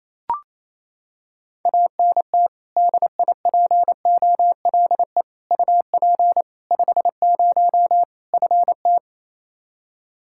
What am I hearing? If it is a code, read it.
Morse code, heard as a keyed tone, about 28 wpm, 700 Hz. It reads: ANT DIPOLE UP 50 FT
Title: Practice QSO - Exchange 2 - Info - 28wpm